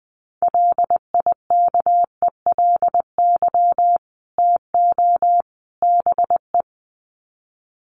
Morse code, heard as a keyed tone, about 20 wpm, 700 Hz. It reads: LIKELY TO BE
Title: Sets of 3 Words 20wpm